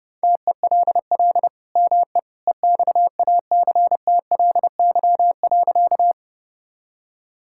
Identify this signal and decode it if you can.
Morse code, heard as a keyed tone, about 32 wpm, 700 Hz. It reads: TELL ME EXACTLY.
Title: Sentences from Top 1000 Words - Review Part 6 - 30wpm